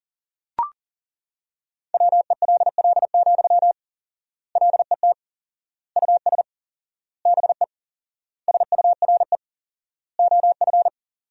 Morse code, heard as a keyed tone, about 40 wpm, 700 Hz. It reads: WELL, LET US BE SURE OF
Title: 4x Word-Spacing - Sentences from Top 300 Words - 40wpm